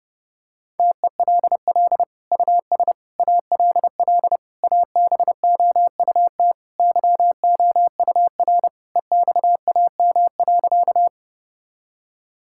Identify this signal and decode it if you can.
Morse code, heard as a keyed tone, about 32 wpm, 700 Hz. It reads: TELL US ALL ABOUT YOUR EXAM.
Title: Sentences from Top 1000 Words - Review Part 5 - 30wpm